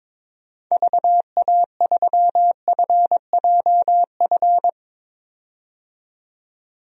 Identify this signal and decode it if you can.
Morse code, heard as a keyed tone, about 22 wpm, 700 Hz. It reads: VA3FJF
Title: DXCC-100 Call Signs - 6 Characters - 22wpm